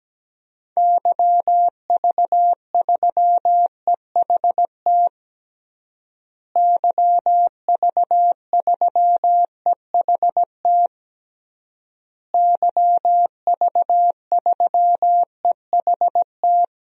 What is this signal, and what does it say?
Morse code, heard as a keyed tone, about 17 wpm, 700 Hz. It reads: YV3EHT YV3EHT YV3EHT
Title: DXCC-100 Call Signs - 6 Characters - Repeated 3x - 17wpm